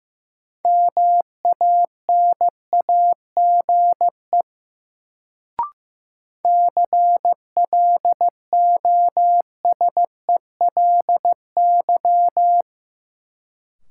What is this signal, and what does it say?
Morse code, heard as a keyed tone, about 15 wpm, 700 Hz. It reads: MANAGE CLOSELY
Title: Sentences from Top 1400 Words - New Word Drill Alt - 15wpm